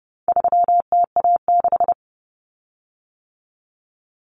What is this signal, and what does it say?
Morse code, heard as a keyed tone, about 30 wpm, 700 Hz. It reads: 3TA6
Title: Common Cut Numbers - 4-Digit 30wpm